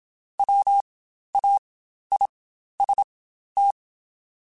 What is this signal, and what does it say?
Morse code, heard as a keyed tone, about 27 wpm, spaced as for 12 wpm, 800 Hz. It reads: WAIST
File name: Spaced / Chuckles Spaced-mp3s / MiddleAge sp @25